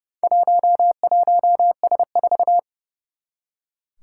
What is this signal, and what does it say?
Morse code, heard as a keyed tone, about 30 wpm, 700 Hz. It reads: 11S4
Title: Four Challenging Letter-Number Combinations 30wpm